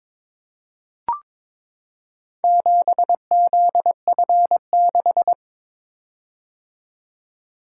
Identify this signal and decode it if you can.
Morse code, heard as a keyed tone, about 22 wpm, 700 Hz. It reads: 7ZF6
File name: practice / 22wpm / Four Letter-Number Combinations - 22wpm